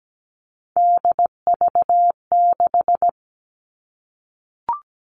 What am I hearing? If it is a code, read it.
Morse code, heard as a keyed tone, about 17 wpm, 700 Hz. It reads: DV6
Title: All Cut Numbers - 3-Digit 17wpm